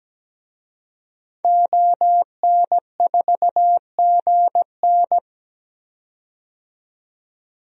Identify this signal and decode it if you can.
Morse code, heard as a keyed tone, about 17 wpm, 700 Hz. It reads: ON4GN
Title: DXCC-100 Call Signs - 5 Characters - 17wpm